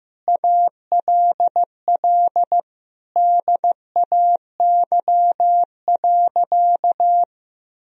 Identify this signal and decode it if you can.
Morse code, heard as a keyed tone, about 16 wpm, 700 Hz. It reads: ALL DAY.
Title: Sentences from Top 1000 Words - Review Part 1 - 15wpm